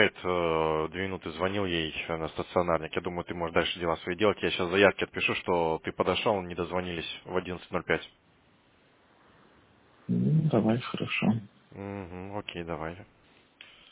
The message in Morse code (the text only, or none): none